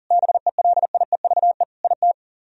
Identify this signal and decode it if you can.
Morse code, heard as a keyed tone, about 52 wpm, 700 Hz. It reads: BELIEVE IT